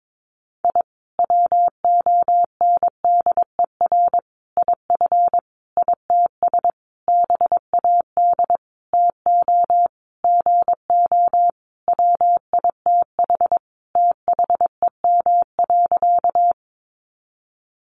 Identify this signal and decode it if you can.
Morse code, heard as a keyed tone, about 22 wpm, 700 Hz. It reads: I WONDER IF ITS BAD TO GO WITH THEM.
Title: Sentences from Top 700 Words 22wpm